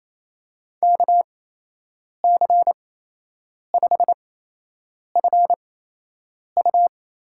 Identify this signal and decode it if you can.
Morse code, heard as a keyed tone, about 27 wpm, spaced as for 8 wpm, 700 Hz. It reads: KC5FU
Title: USA Call Signs - 6 Characters - 8x Character Spacing - 28wpm